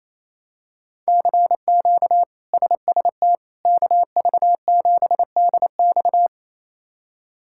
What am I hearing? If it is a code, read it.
Morse code, heard as a keyed tone, about 28 wpm, 700 Hz. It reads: CQ SST KV7DX
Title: K1USN SST QSO Practice - 28wpm